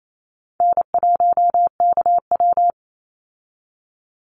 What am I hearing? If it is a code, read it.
Morse code, heard as a keyed tone, about 28 wpm, 700 Hz. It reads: N1KW
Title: Call Signs Repeated 2x - 28wpm